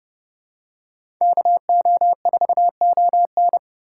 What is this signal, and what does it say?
Morse code, heard as a keyed tone, about 30 wpm, 700 Hz. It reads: KO4ON
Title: USA Call Signs - 6x Character Spacing - 30wpm